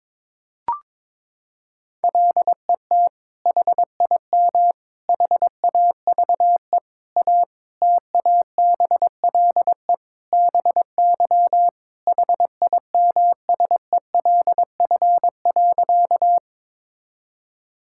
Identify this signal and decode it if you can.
Morse code, heard as a keyed tone, about 22 wpm, 700 Hz. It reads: LET HIM HAVE A TABLE BY HIMSELF.
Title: Sentences from Top 1600 Words 22wpm